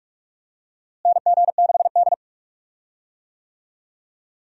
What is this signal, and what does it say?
Morse code, heard as a keyed tone, about 46 wpm, 700 Hz. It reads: NG6D
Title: USA Call Signs - 4 Characters - Repeated 3x - 45wpm